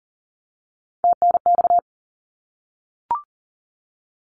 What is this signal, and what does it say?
Morse code, heard as a keyed tone, about 38 wpm, 700 Hz. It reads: TNX
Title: Abbreviations 40wpm